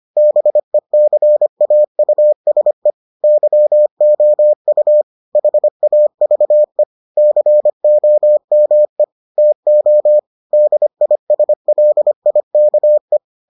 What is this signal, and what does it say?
Morse code, heard as a keyed tone, about 25 wpm, 600 Hz. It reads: BECAUSE YOU HAVE COME TO DISLIKE